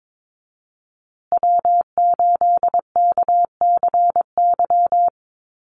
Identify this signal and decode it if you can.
Morse code, heard as a keyed tone, about 22 wpm, 700 Hz. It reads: W8KCY
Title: USA Call Signs Repeated 2x 22wpm